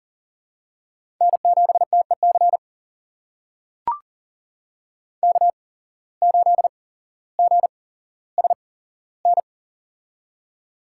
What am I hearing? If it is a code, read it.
Morse code, heard as a keyed tone, about 40 wpm, spaced as for 17 wpm, 700 Hz. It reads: N7TEC K8GSN
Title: USA Call Signs - 5 Characters - 8x Character Spacing - 40wpm